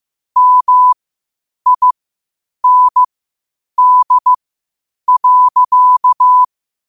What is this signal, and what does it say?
Morse code, heard as a keyed tone, about 15 wpm, spaced as for 8 wpm, 1000 Hz. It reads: MIND.
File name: qotdfiles / 8wpm